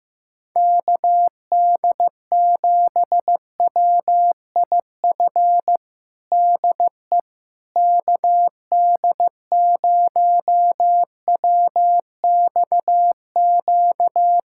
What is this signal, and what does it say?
Morse code, heard as a keyed tone, about 16 wpm, 700 Hz. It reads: KD7WIF DE KD0WXQ
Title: USA Callsign DE Callsign - 6 Characters - 15wpm